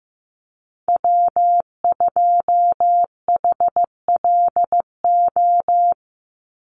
Morse code, heard as a keyed tone, about 15 wpm, 700 Hz. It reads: W2HLO
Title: USA Call Signs 15wpm